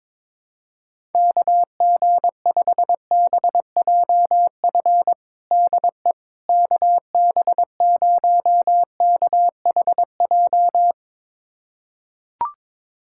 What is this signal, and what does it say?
Morse code, heard as a keyed tone, about 22 wpm, 700 Hz. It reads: KG5BJF DE KB0KHJ
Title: USA Callsign DE Callsign - 6 Characters - 22wpm